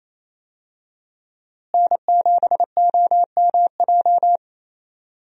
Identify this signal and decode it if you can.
Morse code, heard as a keyed tone, about 28 wpm, 700 Hz. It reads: N7OMJ
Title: USA Call Signs - 5 Characters - Repeated 2x - 28wpm